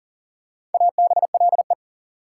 Morse code, heard as a keyed tone, about 40 wpm, 700 Hz. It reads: ABLE
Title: Sentences from Top 700 Words - New Word Drill - 40wpm